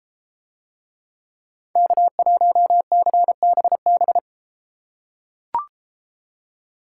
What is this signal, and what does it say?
Morse code, heard as a keyed tone, about 33 wpm, 700 Hz. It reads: K1CBB